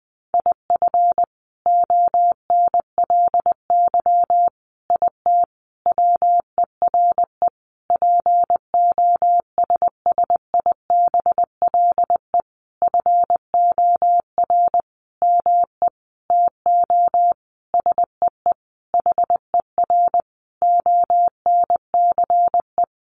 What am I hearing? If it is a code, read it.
Morse code, heard as a keyed tone, about 19 wpm, 700 Hz. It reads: IF ONLY IT WERE POSSIBLE FOR ME TO SEE HER ONCE
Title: Sentences from Top 500 Words 20wpm